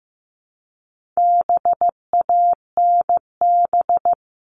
Morse code, heard as a keyed tone, about 15 wpm, 700 Hz. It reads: BANB
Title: All Cut Numbers - 4-Digit 15wpm